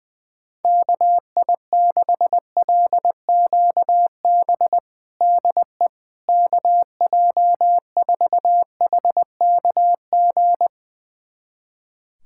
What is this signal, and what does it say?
Morse code, heard as a keyed tone, about 21 wpm, 700 Hz. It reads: KI6LQB DE KJ4HKG
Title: USA Callsign DE Callsign - 6 Characters - 20wpm